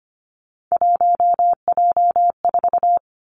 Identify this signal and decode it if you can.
Morse code, heard as a keyed tone, about 25 wpm, 700 Hz. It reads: 1J4
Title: Three Challenging Letter-Number Combinations 25wpm